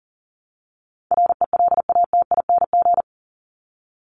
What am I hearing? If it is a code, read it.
Morse code, heard as a keyed tone, about 40 wpm, 700 Hz. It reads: RELATING
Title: ING Suffix - Top 100 Words with Base Words 40wpm